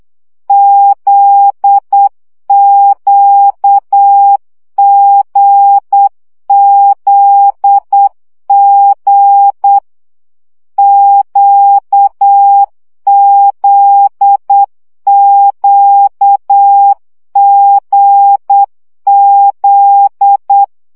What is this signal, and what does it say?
Morse code, heard as a keyed tone, about 8 wpm, 800 Hz. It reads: ZQGZG QZQGZ
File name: CW gyakorlas g z q 7 baud